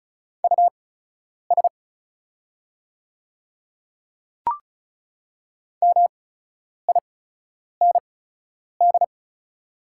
Morse code, heard as a keyed tone, about 36 wpm, spaced as for 10 wpm, 700 Hz. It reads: US MIND
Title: 8x Character Spacing - Top 500 Words - 35wpm